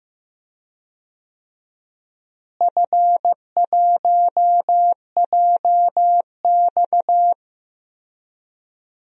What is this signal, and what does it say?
Morse code, heard as a keyed tone, about 15 wpm, 700 Hz. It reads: F1JX
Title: DXCC-100 Call Signs - 4 Characters - Repeated 3x - 15wpm